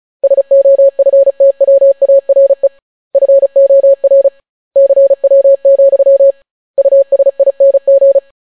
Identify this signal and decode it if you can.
Morse code, heard as a keyed tone, about 36 wpm, 550 Hz. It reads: SOFTWARE FOR CW, USING